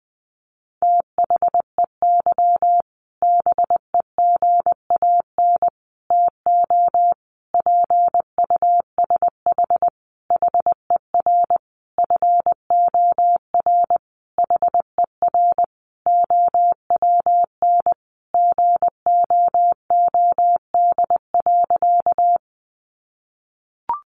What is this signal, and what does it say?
Morse code, heard as a keyed tone, about 20 wpm, 700 Hz. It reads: THEY BEGAN TO PUSH HER FOR HER OWN GOOD.
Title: Sentences from Top 1300 Words 20wpm